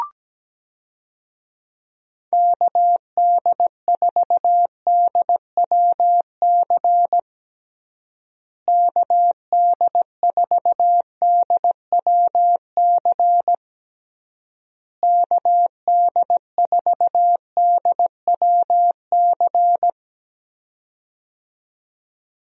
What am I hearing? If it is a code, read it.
Morse code, heard as a keyed tone, about 17 wpm, 700 Hz. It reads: KD4DWC KD4DWC KD4DWC
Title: USA Call Signs - 6 Characters - Repeated 3x - 17wpm